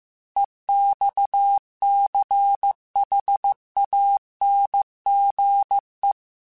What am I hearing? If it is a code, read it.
Morse code, heard as a keyed tone, about 15 wpm, 800 Hz. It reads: EXCHANGE
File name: QSOs / QSOs-mp3s / WA7QOX 1@13